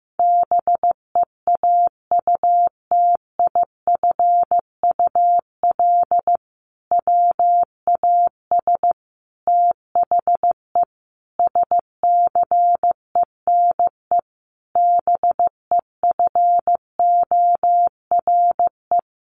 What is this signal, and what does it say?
Morse code, heard as a keyed tone, about 15 wpm, 700 Hz. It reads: BEAUTIFUL WAS THE SCENE BEFORE
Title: Sentences from Top 1100 Words 15wpm